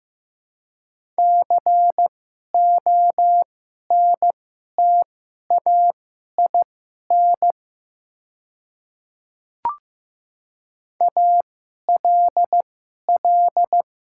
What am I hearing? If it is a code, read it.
Morse code, heard as a keyed tone, about 15 wpm, spaced as for 11 wpm, 690 Hz. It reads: CONTAIN ALL
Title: 2x Character Spacing - Top 500 Words - 15wpm